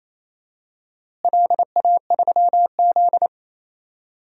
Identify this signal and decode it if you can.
Morse code, heard as a keyed tone, about 28 wpm, 700 Hz. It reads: LA3Z
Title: DXCC-100 Call Signs - 4 Characters - Repeated 3x - 28wpm